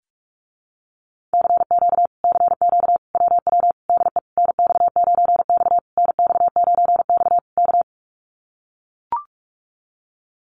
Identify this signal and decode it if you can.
Morse code, heard as a keyed tone, about 44 wpm, 700 Hz. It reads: CQ CQ WW DE NX9X NX9X K
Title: Common QSO Phrases Encore 45wpm